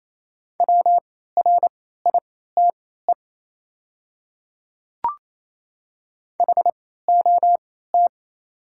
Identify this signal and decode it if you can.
Morse code, heard as a keyed tone, about 28 wpm, spaced as for 16 wpm, 700 Hz. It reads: WRITE HOT
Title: 3x Character Spacing - Top 100 Words - 28wpm